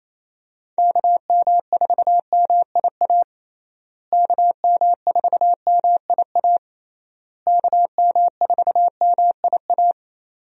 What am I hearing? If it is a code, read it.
Morse code, heard as a keyed tone, about 28 wpm, 700 Hz. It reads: KM4MIA KM4MIA KM4MIA
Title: USA Call Signs - 6 Characters - Repeated 3x - 28wpm